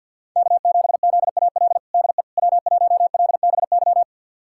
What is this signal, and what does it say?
Morse code, heard as a keyed tone, about 47 wpm, 700 Hz. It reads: K7ZAL DE W1LBY
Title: USA Callsign DE Callsign - 5 Characters - 50wpm